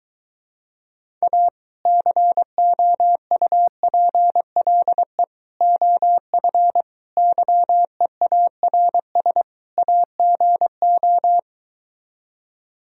Morse code, heard as a keyed tone, about 23 wpm, 700 Hz. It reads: A COUPLE OF YEARS AGO